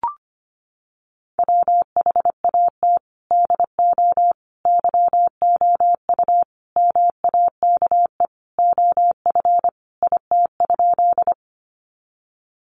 Morse code, heard as a keyed tone, about 25 wpm, 700 Hz. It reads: WHAT DO YOU MAKE OF IT?
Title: Sentences from Top 100 Words 25wpm